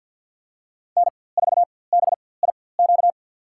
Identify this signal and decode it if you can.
Morse code, heard as a keyed tone, about 48 wpm, spaced as for 23 wpm, 700 Hz. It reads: N4BIQ
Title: USA Call Signs - 5 Characters - 4x Character Spacing - 50wpm